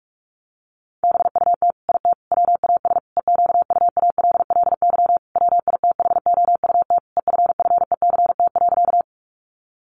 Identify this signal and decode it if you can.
Morse code, heard as a keyed tone, about 44 wpm, 700 Hz. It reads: BUT IT WAS EQUALLY WITHOUT EFFECT.
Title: Sentences from Top 1400 Words 45wpm